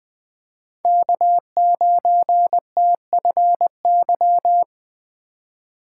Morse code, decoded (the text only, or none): K9TFY